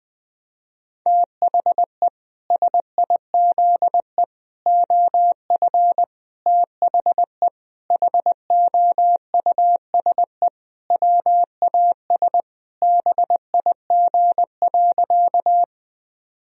Morse code, heard as a keyed tone, about 21 wpm, 700 Hz. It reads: THE SIZE OF THE HOUSE WAS BIG.
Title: Sentences from Top 1000 Words - Review Part 2 - 20wpm